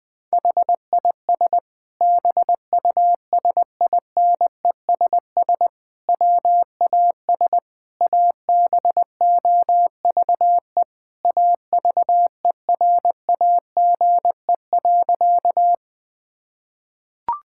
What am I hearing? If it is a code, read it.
Morse code, heard as a keyed tone, about 21 wpm, 700 Hz. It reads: HIS BUSINESS WAS ABOVE AVERAGE.
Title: Sentences from Top 1000 Words - Review Part 6 - 20wpm